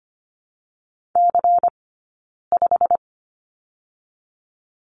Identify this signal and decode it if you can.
Morse code, heard as a keyed tone, about 24 wpm, 700 Hz. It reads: C 5
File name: Lesson 17d - Letter u - Two Characters 25wpm